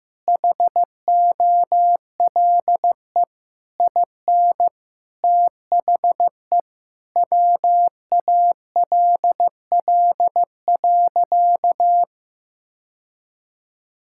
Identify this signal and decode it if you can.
Morse code, heard as a keyed tone, about 15 wpm, 700 Hz. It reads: HOLE IN THE WALL.
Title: Sentences from Top 1600 Words 15wpm